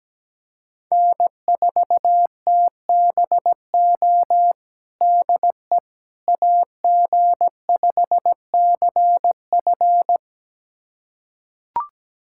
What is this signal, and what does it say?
Morse code, heard as a keyed tone, about 17 wpm, 700 Hz. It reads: N4TBO DE AG5CF
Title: USA Callsign DE Callsign - 5 Characters - 17wpm